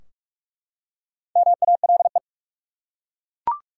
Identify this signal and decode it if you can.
Morse code, heard as a keyed tone, about 42 wpm, 700 Hz. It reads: MALE
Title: Four Letter Words - 45wpm